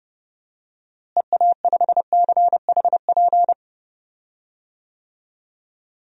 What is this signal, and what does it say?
Morse code, heard as a keyed tone, about 30 wpm, 700 Hz. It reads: EA5CHP
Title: DXCC-100 Call Signs - 6 Characters - Repeated 2x - 30wpm